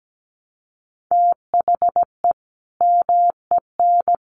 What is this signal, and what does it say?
Morse code, heard as a keyed tone, about 17 wpm, 700 Hz. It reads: THE MEN
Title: Sentences from Top 1200 Words 17wpm